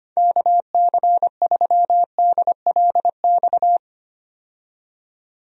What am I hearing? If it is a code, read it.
Morse code, heard as a keyed tone, about 25 wpm, 700 Hz. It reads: KC3DLX
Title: USA Call Signs - 6 Characters - 25wpm